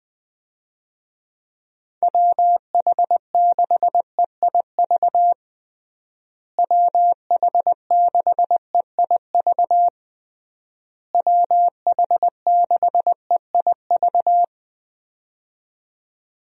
Morse code, heard as a keyed tone, about 21 wpm, 700 Hz. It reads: WH6EIV WH6EIV WH6EIV
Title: USA Call Signs - 6 Characters - Repeated 3x - 20wpm